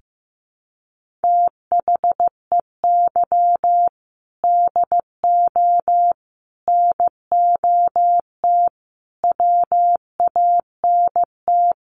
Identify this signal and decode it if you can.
Morse code, heard as a keyed tone, about 15 wpm, 700 Hz. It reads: THEY DO NOT WANT
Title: Sets of 4 Words 15wpm